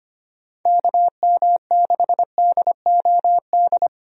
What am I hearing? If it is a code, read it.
Morse code, heard as a keyed tone, about 25 wpm, 700 Hz. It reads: KM6DOD
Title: USA Call Signs - 6 Characters - 25wpm